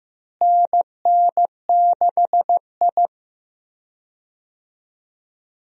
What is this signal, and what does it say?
Morse code, heard as a keyed tone, about 16 wpm, 700 Hz. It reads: NN6I
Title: USA Call Signs - 4 Characters - Repeated 3x - 15wpm